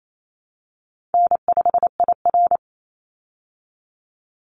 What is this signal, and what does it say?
Morse code, heard as a keyed tone, about 28 wpm, 700 Hz. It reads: N5IR
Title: Easy Call Signs Repeated 2x - 28wpm